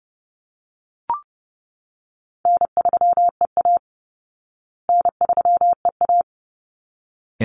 Morse code, heard as a keyed tone, about 30 wpm, 700 Hz. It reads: N3EA N3EA
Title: Easy Call Signs Repeated 2x - 30wpm